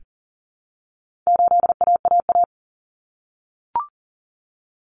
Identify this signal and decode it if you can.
Morse code, heard as a keyed tone, about 38 wpm, 700 Hz. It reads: 8AAA
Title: Common Cut Numbers - 4-Digit 40wpm